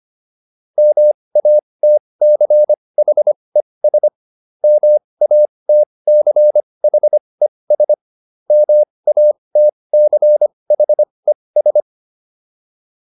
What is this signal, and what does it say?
Morse code, heard as a keyed tone, about 25 wpm, spaced as for 20 wpm, 600 Hz. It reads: MATCHES MATCHES MATCHES